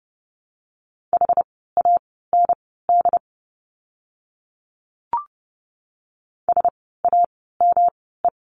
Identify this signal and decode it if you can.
Morse code, heard as a keyed tone, about 30 wpm, spaced as for 17 wpm, 700 Hz. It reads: HAND SAME
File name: course / 30:10wpm / 115 Lesson 19f - Letter w - Words 30wpm